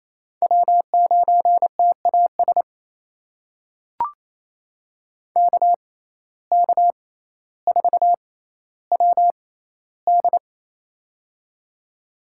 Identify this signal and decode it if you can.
Morse code, heard as a keyed tone, about 28 wpm, spaced as for 14 wpm, 700 Hz. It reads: W9TAS KK4WD
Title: USA Call Signs - 5 Characters - 6x Character Spacing - 28wpm